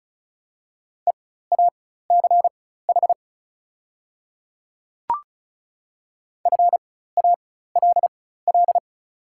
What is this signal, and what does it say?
Morse code, heard as a keyed tone, about 34 wpm, spaced as for 16 wpm, 700 Hz. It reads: EACH FALL
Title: 4x Character Spacing - Top 500 Words - 35wpm